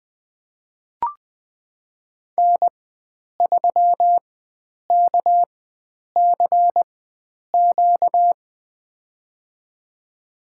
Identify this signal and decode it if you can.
Morse code, heard as a keyed tone, about 20 wpm, spaced as for 9 wpm, 700 Hz. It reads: N3KCQ
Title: USA Call Signs - 5 Characters - 4x Character Spacing - 20wpm